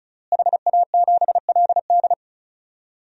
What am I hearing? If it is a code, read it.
Morse code, heard as a keyed tone, about 36 wpm, 700 Hz. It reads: HA7LD
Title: DXCC-100 Call Signs - 5 Characters - Repeated 3x - 35wpm